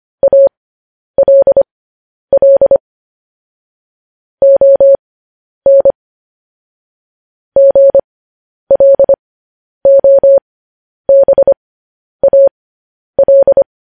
Morse code, heard as a keyed tone, about 25 wpm, spaced as for 10 wpm, 550 Hz. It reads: ALL ON GLOBAL
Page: News Headlines in Morse Code at 10 WPM